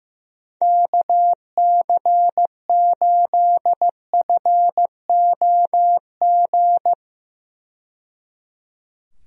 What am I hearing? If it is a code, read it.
Morse code, heard as a keyed tone, about 15 wpm, 700 Hz. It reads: KC8FOG
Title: USA Call Signs - 6 Characters - 15wpm